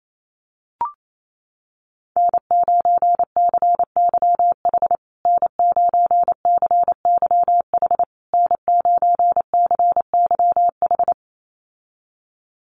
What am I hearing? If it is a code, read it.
Morse code, heard as a keyed tone, about 28 wpm, 690 Hz. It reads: N9CYH N9CYH N9CYH
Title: USA Call Signs Repeated 3x 28wpm